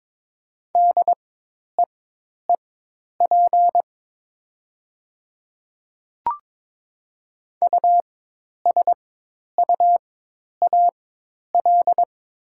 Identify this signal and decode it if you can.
Morse code, heard as a keyed tone, about 22 wpm, spaced as for 10 wpm, 700 Hz. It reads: DEEP USUAL
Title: 4x Character Spacing - Top 500 Words - 22wpm